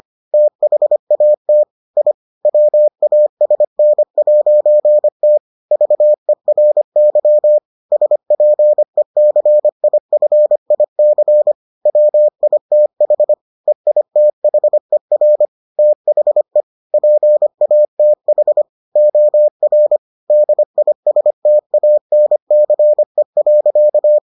Morse code, heard as a keyed tone, about 25 wpm, 600 Hz. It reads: THAT I WASN'T VERY SPECIFIC WITH EITHER THE PATH OR DISTANCE.